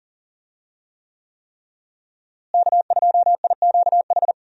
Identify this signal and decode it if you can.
Morse code, heard as a keyed tone, about 39 wpm, 700 Hz. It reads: K2IQH
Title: USA Call Signs - 5 Characters - Repeated 3x - 40wpm